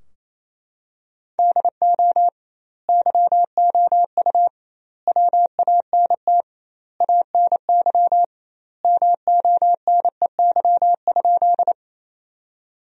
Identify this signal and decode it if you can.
Morse code, heard as a keyed tone, about 28 wpm, 700 Hz. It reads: DO YOU WANT ANY MONEY?
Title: 2x Word-Spacing - Sentences from Top 400 Words - 28wpm